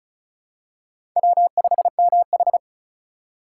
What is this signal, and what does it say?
Morse code, heard as a keyed tone, about 36 wpm, 690 Hz. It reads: W5MH
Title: USA Call Signs - 4 Characters - 2x Character Spacing - 35wpm